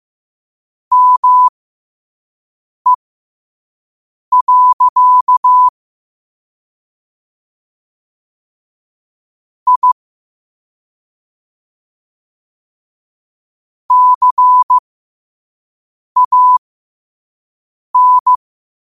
Morse code, heard as a keyed tone, about 15 wpm, spaced as for 5 wpm, 1000 Hz. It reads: ME. I CAN